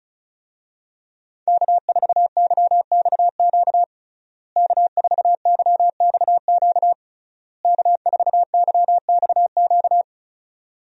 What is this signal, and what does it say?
Morse code, heard as a keyed tone, about 36 wpm, 700 Hz. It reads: K4YXQ K4YXQ K4YXQ
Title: USA Call Signs - 5 Characters - Repeated 3x - 35wpm